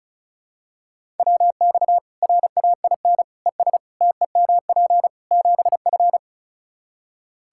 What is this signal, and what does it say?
Morse code, heard as a keyed tone, about 34 wpm, 700 Hz. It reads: WX RAIN ES TEMP 7F
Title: Practice QSO - Exchange 2 - Info - 35wpm